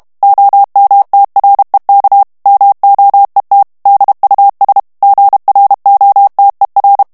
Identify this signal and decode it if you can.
Morse code, heard as a keyed tone, about 32 wpm, 800 Hz. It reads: OMTREK MOET DUS GROTER
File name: morse mp3 / HOLLANDAIS texte clair 1 vitesse 30 mots minute